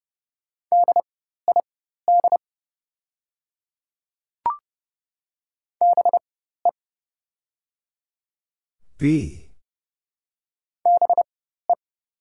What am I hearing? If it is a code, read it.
Morse code, heard as a keyed tone, about 30 wpm, spaced as for 14 wpm, 700 Hz. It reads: DID BE BE
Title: 4x Character Spacing - Top 100 Words - 30wpm